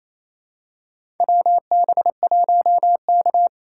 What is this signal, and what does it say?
Morse code, heard as a keyed tone, about 28 wpm, 700 Hz. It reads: WB1K
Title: USA Call Signs - 4 Characters - 3x Character Spacing - 28wpm